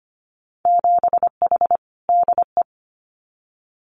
Morse code, heard as a keyed tone, about 25 wpm, 700 Hz. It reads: 7H DE
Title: Winter Field Day Exchanges Encore 25wpm